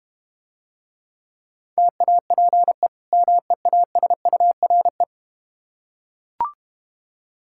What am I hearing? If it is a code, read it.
Morse code, heard as a keyed tone, about 30 wpm, 700 Hz. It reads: TAPE MEASURE
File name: Easy Words - Speed-Racing with Hints - 25wpm